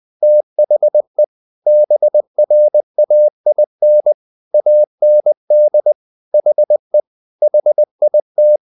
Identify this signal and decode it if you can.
Morse code, heard as a keyed tone, about 20 wpm, 600 Hz. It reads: THE BRAIN AND HE HIT
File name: the old man and the sea part 2 20wpm